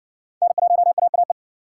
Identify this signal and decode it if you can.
Morse code, heard as a keyed tone, about 60 wpm, 700 Hz. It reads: N1AAE